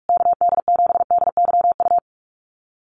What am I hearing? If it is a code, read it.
Morse code, heard as a keyed tone, about 44 wpm, 700 Hz. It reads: KD7DYU
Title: USA Call Signs Repeated 3x 45wpm